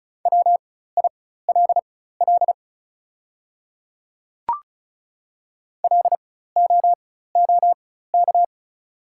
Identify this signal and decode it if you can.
Morse code, heard as a keyed tone, about 35 wpm, spaced as for 16 wpm, 690 Hz. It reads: WILL LOOK
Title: 4x Character Spacing - Top 100 Words - 35wpm